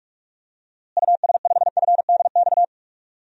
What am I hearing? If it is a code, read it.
Morse code, heard as a keyed tone, about 46 wpm, 700 Hz. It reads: US5FDX
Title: DXCC-100 Call Signs - 6 Characters - Repeated 2x - 45wpm